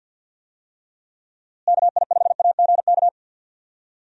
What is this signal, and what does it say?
Morse code, heard as a keyed tone, about 53 wpm, 700 Hz. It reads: KI5AGK
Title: USA Call Signs - 6 Characters - Repeated 2x - 50wpm